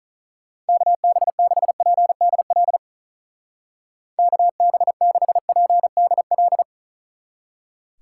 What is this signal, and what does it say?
Morse code, heard as a keyed tone, about 47 wpm, 700 Hz. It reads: KB6PDL KB6PDL